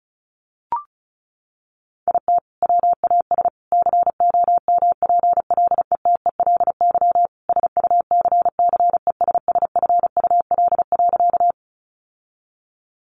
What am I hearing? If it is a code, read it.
Morse code, heard as a keyed tone, about 34 wpm, 700 Hz. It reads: IT WAS COMPLETELY SUCCESSFUL.
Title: Sentences from Top 1100 Words 35wpm